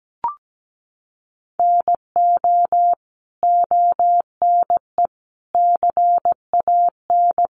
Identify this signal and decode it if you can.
Morse code, heard as a keyed tone, about 17 wpm, 700 Hz. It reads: NO ONE CAN
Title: Sentences from Top 1400 Words 17wpm